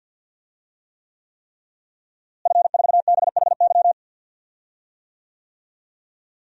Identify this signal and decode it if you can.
Morse code, heard as a keyed tone, about 47 wpm, 700 Hz. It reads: F4BHY
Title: DXCC-100 Call Signs - 5 Characters - Repeated 3x - 50wpm